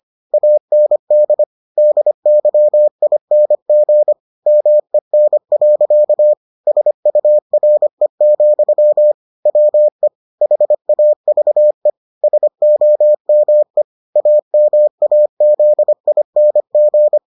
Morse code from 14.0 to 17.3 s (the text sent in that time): AMAZING